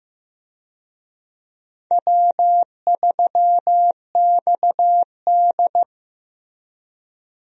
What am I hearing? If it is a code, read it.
Morse code, heard as a keyed tone, about 15 wpm, 700 Hz. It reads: W3XD
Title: USA Call Signs - 4 Characters - Repeated 3x - 15wpm